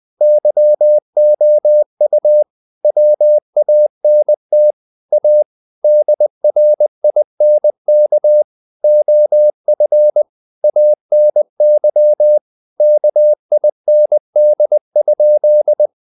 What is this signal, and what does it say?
Morse code, heard as a keyed tone, about 20 wpm, 600 Hz. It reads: YOU WANT A DRINK OF ANY KIND?
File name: the old man and the sea part 2 20wpm